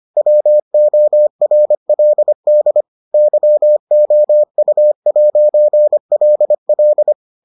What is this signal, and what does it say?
Morse code, heard as a keyed tone, about 25 wpm, 600 Hz. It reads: WORLD YOU'LL